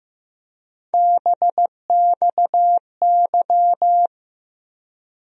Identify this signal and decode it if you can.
Morse code, heard as a keyed tone, about 15 wpm, 700 Hz. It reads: BXY
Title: USA Call Sign Postfixes - 3 Letters 15wpm